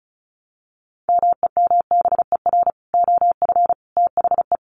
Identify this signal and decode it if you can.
Morse code, heard as a keyed tone, about 34 wpm, 700 Hz. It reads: MEMBER OF THE
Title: Sets of 3 Words 35wpm